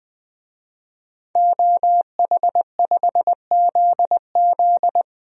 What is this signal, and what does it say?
Morse code, heard as a keyed tone, about 21 wpm, 700 Hz. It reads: OH5ZZ
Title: DXCC-100 Call Signs - 5 Characters - Repeated 3x - 20wpm